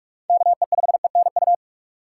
Morse code, heard as a keyed tone, about 46 wpm, 700 Hz. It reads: KE5ENU
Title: USA Call Signs - 6 Characters - 6x Character Spacing - 45wpm